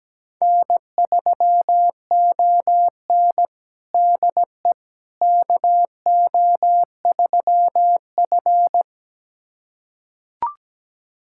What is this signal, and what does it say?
Morse code, heard as a keyed tone, about 17 wpm, 700 Hz. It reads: N3ON DE KO3F
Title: USA Callsign DE Callsign - 4 Characters - 17wpm